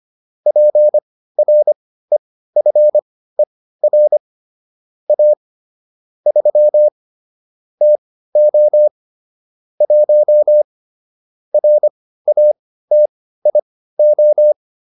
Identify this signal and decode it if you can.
Morse code, heard as a keyed tone, about 25 wpm, spaced as for 15 wpm, 600 Hz. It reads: PREFER A 3 TO 1 RATIO